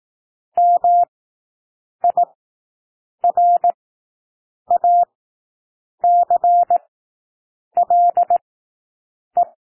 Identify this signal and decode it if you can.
Morse code, heard as a keyed tone, about 18 wpm, spaced as for 7 wpm, 700 Hz. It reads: MIRACLE
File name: SOUND WORDS 18 wpm